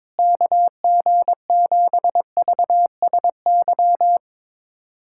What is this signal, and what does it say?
Morse code, heard as a keyed tone, about 22 wpm, 700 Hz. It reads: KG7VSY
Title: USA Callsign DE Callsign 22wpm